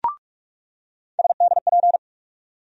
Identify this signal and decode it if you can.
Morse code, heard as a keyed tone, about 44 wpm, 700 Hz. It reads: SDP
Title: USA Call Sign Postfixes - 3 Letters 45wpm